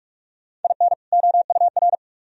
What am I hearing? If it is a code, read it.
Morse code, heard as a keyed tone, about 42 wpm, 700 Hz. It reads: IN OUR